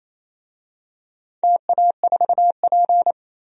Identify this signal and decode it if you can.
Morse code, heard as a keyed tone, about 28 wpm, 700 Hz. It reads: TA4P
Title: DXCC-100 Call Signs - 4 Characters - 28wpm